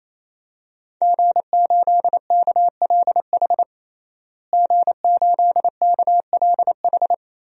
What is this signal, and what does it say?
Morse code, heard as a keyed tone, about 28 wpm, 700 Hz. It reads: G8KLH G8KLH
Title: DXCC-100 Call Signs - 5 Characters - Repeated 3x - 28wpm